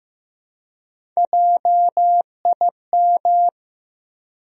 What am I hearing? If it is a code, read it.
Morse code, heard as a keyed tone, about 15 wpm, 700 Hz. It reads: JIM
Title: USA Common Names - Max 4 Letters - 15wpm